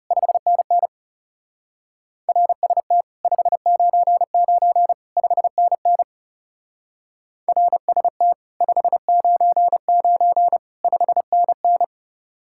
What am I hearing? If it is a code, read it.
Morse code, heard as a keyed tone, about 36 wpm, 700 Hz. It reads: 5NN RST 599 5NN RST 599 5NN
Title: Speed-Racing Common QSO Phrases - 30wpm